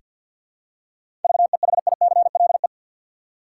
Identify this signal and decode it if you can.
Morse code, heard as a keyed tone, about 48 wpm, 700 Hz. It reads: VEHICLE
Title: Seven Letter Words - 50wpm